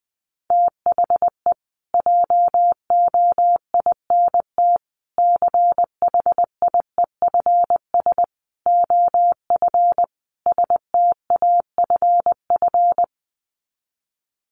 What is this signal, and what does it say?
Morse code, heard as a keyed tone, about 20 wpm, 700 Hz. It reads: THE JOINT CHIEFS OF STAFF
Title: Sets of 5 Words Encore2 20wpm